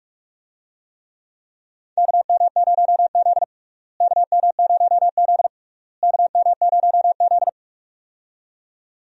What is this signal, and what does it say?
Morse code, heard as a keyed tone, about 46 wpm, 700 Hz. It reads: KM0Z KM0Z KM0Z
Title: USA Call Signs - 4 Characters - Repeated 3x - 45wpm